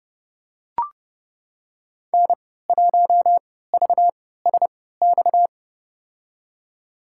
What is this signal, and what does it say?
Morse code, heard as a keyed tone, about 30 wpm, spaced as for 17 wpm, 700 Hz. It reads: N1VSX
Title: USA Call Signs - 5 Characters - 3x Character Spacing - 30wpm